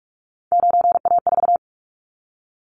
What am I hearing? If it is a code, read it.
Morse code, heard as a keyed tone, about 46 wpm, 700 Hz. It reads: 9A4
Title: Callsign Prefixes for Top 25 DXCC - 45wpm